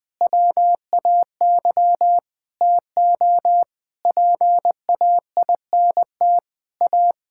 Morse code, heard as a keyed tone, about 20 wpm, 700 Hz. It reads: WAY TO PAINT A
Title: Sentences with 8-Word Phrases - Spoken Beginning - 20wpm